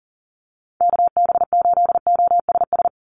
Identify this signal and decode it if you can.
Morse code, heard as a keyed tone, about 37 wpm, 690 Hz. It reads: KB8OSS
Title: USA Call Signs 40wpm